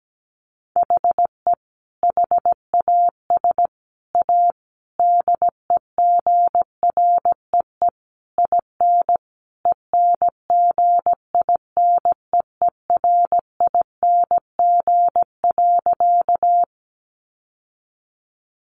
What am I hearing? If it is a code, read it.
Morse code, heard as a keyed tone, about 17 wpm, 700 Hz. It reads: HE HAS A DEGREE IN ENGINEERING.
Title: Sentences from Top 1500 Words 17wpm